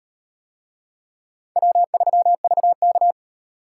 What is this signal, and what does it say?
Morse code, heard as a keyed tone, about 38 wpm, 700 Hz. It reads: W3VK